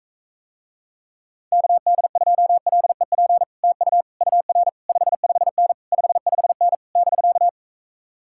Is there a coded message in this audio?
Yes: Morse code from 1.5 to 7.5 s, heard as a keyed tone, about 44 wpm, 700 Hz. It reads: KD2LEP TU UR 55N 55N <BK>